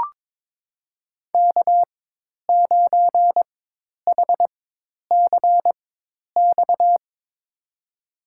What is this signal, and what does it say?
Morse code, heard as a keyed tone, about 22 wpm, spaced as for 10 wpm, 700 Hz. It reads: K9HCX
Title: USA Call Signs - 5 Characters - 4x Character Spacing - 22wpm